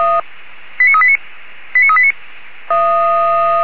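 Five-tone selective call (ZVEI), heard as quick runs of five tones, tones of about 80 ms. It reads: 89389 89389
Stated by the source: ZVEI five-tone selective call received on SDR.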